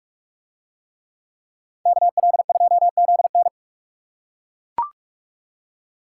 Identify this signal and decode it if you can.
Morse code, heard as a keyed tone, about 41 wpm, 700 Hz. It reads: KL2ZN